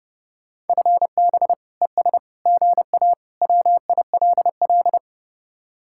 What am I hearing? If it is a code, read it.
Morse code, heard as a keyed tone, about 30 wpm, 700 Hz. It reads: FB ES GA WILL